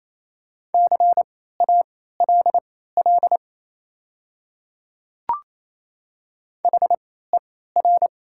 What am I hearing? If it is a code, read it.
Morse code, heard as a keyed tone, about 28 wpm, spaced as for 16 wpm, 690 Hz. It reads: CALL HER
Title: 3x Character Spacing - Top 200 Words - 28wpm